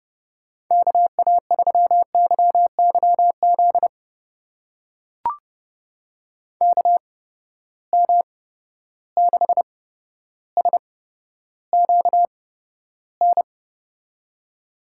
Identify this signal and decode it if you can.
Morse code, heard as a keyed tone, about 30 wpm, spaced as for 13 wpm, 700 Hz. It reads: KA3YYZ KM6SQN
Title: USA Call Signs - 6 Characters - 8x Character Spacing - 30wpm